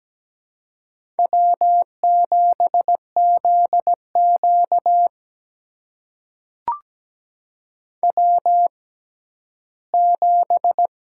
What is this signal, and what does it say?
Morse code, heard as a keyed tone, about 17 wpm, spaced as for 11 wpm, 700 Hz. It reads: W7ZQ W7
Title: USA Call Signs - 6x Character Spacing - 17wpm